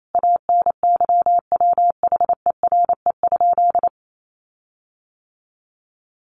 Morse code, heard as a keyed tone, about 28 wpm, 700 Hz. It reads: ANYWHERE?
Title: Sentences from Top 1400 Words 28wpm